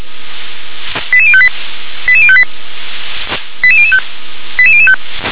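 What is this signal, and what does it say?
Five-tone selective call (ZVEI), heard as quick runs of five tones, tones of about 70 ms. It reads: 88058 88058 88005 88005
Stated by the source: ZVEI five-tone selective call 88104, received by SDR.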